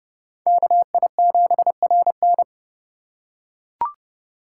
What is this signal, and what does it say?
Morse code, heard as a keyed tone, about 29 wpm, 700 Hz. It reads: KI7RN